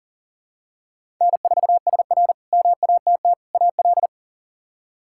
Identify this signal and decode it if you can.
Morse code, heard as a keyed tone, about 40 wpm, 700 Hz. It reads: N4SR MATT AL
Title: K1USN SST QSO Practice - 40wpm